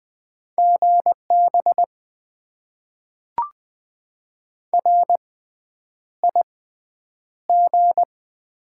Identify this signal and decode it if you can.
Morse code, heard as a keyed tone, about 20 wpm, spaced as for 9 wpm, 700 Hz. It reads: GB RIG
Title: Top 90 QSO Elements - 6x Character Spacing - 20wpm